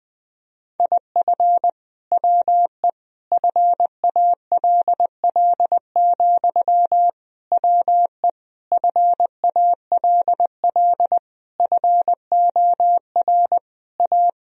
Morse code, heard as a keyed tone, about 20 wpm, 700 Hz. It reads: IF WE FALL, WE FALL FOR A
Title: Sentences from Top 1000 Words - Review Part 2 - 20wpm